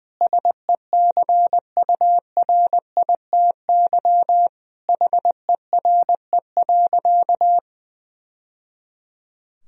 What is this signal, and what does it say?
Morse code, heard as a keyed tone, about 20 wpm, 700 Hz. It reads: SECURITY HERE.
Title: Sentences from Top 1000 Words - Review Part 4 - 20wpm